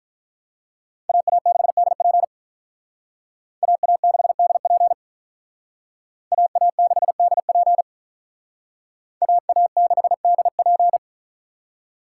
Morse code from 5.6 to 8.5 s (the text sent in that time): AA6DP